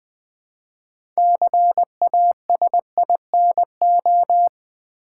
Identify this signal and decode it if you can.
Morse code, heard as a keyed tone, about 20 wpm, 700 Hz. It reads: CASINO
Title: Build A Word - 3 Letters At A Time - 6-Letter Words - 20wpm